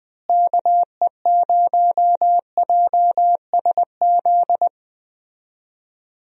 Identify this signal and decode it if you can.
Morse code, heard as a keyed tone, about 20 wpm, 700 Hz. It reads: KE0JSZ
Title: USA Call Signs - 6 Characters - Repeated 2x - 20wpm